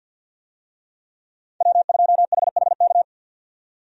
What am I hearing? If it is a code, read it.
Morse code, heard as a keyed tone, about 50 wpm, 700 Hz. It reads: W2HHK
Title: USA Call Signs - 5 Characters - 50wpm